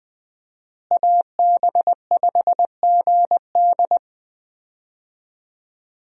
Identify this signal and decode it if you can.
Morse code, heard as a keyed tone, about 21 wpm, 700 Hz. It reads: AB5GD
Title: USA Call Signs - 5 Characters - Repeated 2x - 20wpm